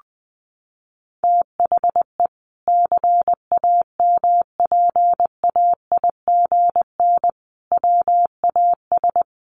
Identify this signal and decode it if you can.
Morse code, heard as a keyed tone, about 20 wpm, 700 Hz. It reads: THE CAMPAIGN WAS
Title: Sentences from Top 1400 Words 20wpm